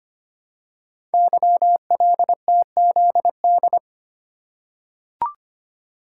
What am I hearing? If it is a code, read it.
Morse code, heard as a keyed tone, about 25 wpm, 700 Hz. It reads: YLTZD
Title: Code Groups - 5 Letter:Numbers - 25wpm